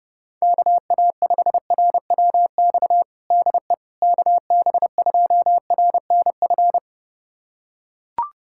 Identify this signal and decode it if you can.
Morse code, heard as a keyed tone, about 30 wpm, 700 Hz. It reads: KA5RWX DE KB2RNF
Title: USA Callsign DE Callsign - 6 Characters - 30wpm